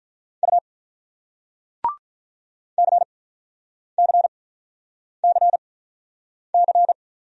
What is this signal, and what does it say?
Morse code, heard as a keyed tone, about 48 wpm, spaced as for 9 wpm, 700 Hz. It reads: U CCCC